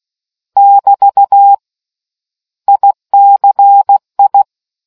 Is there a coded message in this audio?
Yes: Morse code from 0.6 to 4.4 s, heard as a keyed tone, about 16 wpm, 800 Hz. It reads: <BT> ICI